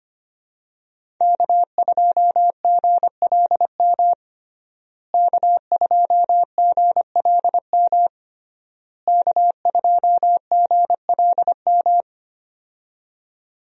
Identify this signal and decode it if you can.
Morse code, heard as a keyed tone, about 25 wpm, 700 Hz. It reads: K2GLM K2GLM K2GLM
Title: USA Call Signs - 5 Characters - Repeated 3x - 25wpm